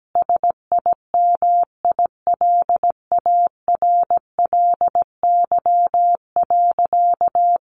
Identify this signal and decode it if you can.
Morse code, heard as a keyed tone, about 17 wpm, 700 Hz. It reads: SIMILARLY.
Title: Sentences from Top 1400 Words 17wpm